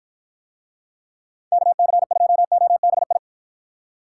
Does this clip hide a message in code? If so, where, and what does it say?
Yes: Morse code from 1.5 to 3.2 s, heard as a keyed tone, about 52 wpm, 700 Hz. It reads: KC2OBI